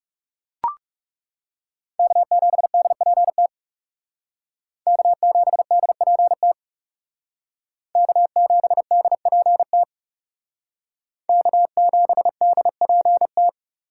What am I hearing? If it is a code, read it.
Morse code, heard as a keyed tone, about 36 wpm, 700 Hz. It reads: K7DPT K7DPT K7DPT K7DPT
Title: Speed Racing - USA 1x3 Call Signs 30wpm